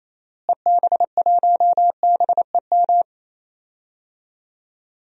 Morse code, heard as a keyed tone, about 28 wpm, 700 Hz. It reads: EB1BEM
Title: DXCC-100 Call Signs - 6 Characters - Repeated 3x - 28wpm